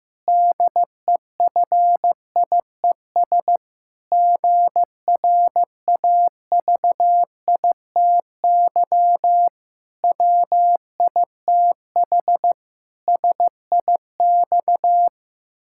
Morse code, heard as a keyed tone, about 16 wpm, 700 Hz. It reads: DEFIES GRAVITY WITH SIX
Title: Pangram Challenge v2 - 15wpm